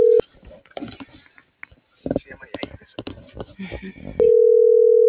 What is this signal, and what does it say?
Ringback tone (North America), 440 + 480 Hz.